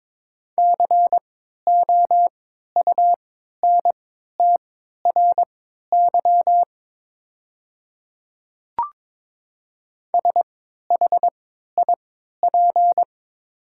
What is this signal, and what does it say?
Morse code, heard as a keyed tone, about 22 wpm, spaced as for 13 wpm, 690 Hz. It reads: COUNTRY SHIP
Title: 3x Character Spacing - Top 500 Words - 22wpm